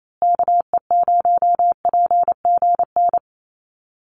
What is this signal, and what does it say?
Morse code, heard as a keyed tone, about 28 wpm, 700 Hz. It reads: KE0PGN
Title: USA Call Signs Repeated 3x 28wpm